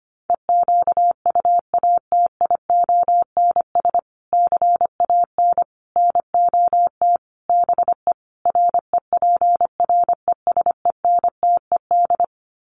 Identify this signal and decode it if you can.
Morse code, heard as a keyed tone, about 25 wpm, 700 Hz. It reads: EQUATIONS CAN NOT BE REPRESENTED
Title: Sets of 5 Words 25wpm